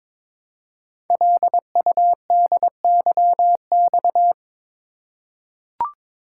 Morse code, heard as a keyed tone, about 22 wpm, 700 Hz. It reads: LUDYX
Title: Code Groups - 5 Letters - 22wpm